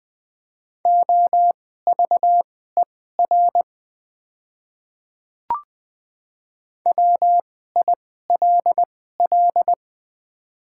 Morse code, heard as a keyed tone, about 20 wpm, spaced as for 14 wpm, 700 Hz. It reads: OVER WILL
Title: 2x Character Spacing - Top 100 Words - 20wpm